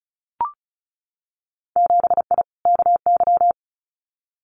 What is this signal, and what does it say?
Morse code, heard as a keyed tone, about 36 wpm, 700 Hz. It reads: 7I KY
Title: Winter Field Day Exchanges Encore 35wpm